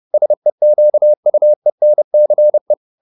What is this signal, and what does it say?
Morse code, heard as a keyed tone, about 30 wpm, 600 Hz. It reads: SEQUENCE